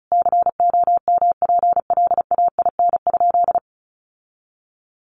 Morse code, heard as a keyed tone, about 34 wpm, 700 Hz. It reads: COMPLAIN?
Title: Sentences from Top 1300 Words 35wpm